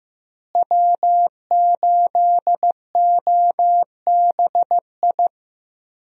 Morse code, heard as a keyed tone, about 15 wpm, 700 Hz. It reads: W8OBI
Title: USA Call Signs - 5 Characters - Repeated 2x - 15wpm